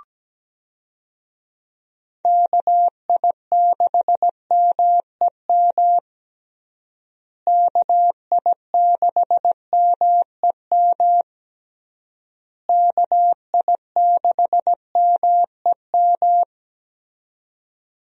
Morse code, heard as a keyed tone, about 17 wpm, 700 Hz. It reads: KI6MEM KI6MEM KI6MEM
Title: USA Call Signs - 6 Characters - Repeated 3x - 17wpm